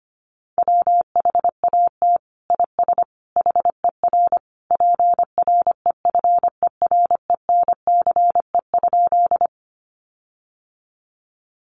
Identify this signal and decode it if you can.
Morse code, heard as a keyed tone, about 25 wpm, 700 Hz. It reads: WHAT IS HER PREFERENCE?
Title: Sentences from Top 1500 Words 25wpm